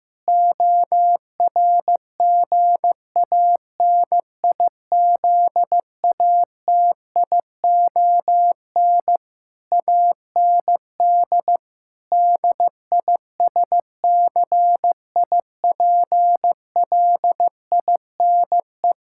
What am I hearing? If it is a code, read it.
Morse code, heard as a keyed tone, about 15 wpm, 700 Hz. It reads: ORGANIZATION AND DISCIPLINE
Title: Sentences from Top 1000 Words - Review Part 6 - 15wpm